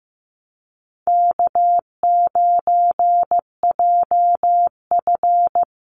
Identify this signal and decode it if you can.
Morse code, heard as a keyed tone, about 15 wpm, 700 Hz. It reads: K9JF
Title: Easy Call Signs - 15wpm